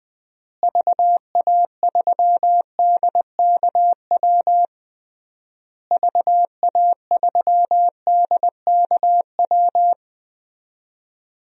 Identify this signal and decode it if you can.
Morse code, heard as a keyed tone, about 20 wpm, 700 Hz. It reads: VA3DKW VA3DKW
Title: DXCC-100 Call Signs - 6 Characters - Repeated 3x - 20wpm